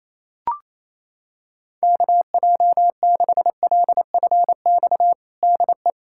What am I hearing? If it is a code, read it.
Morse code, heard as a keyed tone, about 27 wpm, 700 Hz. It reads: KJ6LFX DE
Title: USA Callsign DE Callsign - 6 Characters - 28wpm